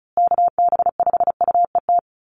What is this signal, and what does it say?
Morse code, heard as a keyed tone, about 34 wpm, 700 Hz. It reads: KB5UET
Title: USA Call Signs Repeated 3x 35wpm